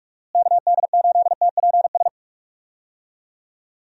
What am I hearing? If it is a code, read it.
Morse code, heard as a keyed tone, about 46 wpm, 700 Hz. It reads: KD8TPS
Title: USA Call Signs - 6 Characters - Repeated 2x - 45wpm